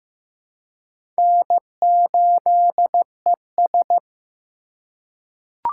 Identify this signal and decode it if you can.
Morse code, heard as a keyed tone, about 15 wpm, 700 Hz. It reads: N8ES
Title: USA Call Signs - 2x Character Spacing - 15wpm